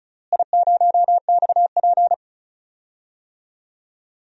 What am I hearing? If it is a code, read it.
Morse code, heard as a keyed tone, about 34 wpm, 700 Hz. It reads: I0XP
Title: DXCC-100 Call Signs - 4 Characters - Repeated 2x - 35wpm